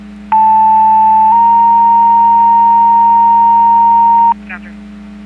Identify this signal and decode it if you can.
Two-tone page, heard as a one-second tone, then a three-second tone, about 880 then 930 Hz, with speech. It is heard from 0.3 to 4.3 s.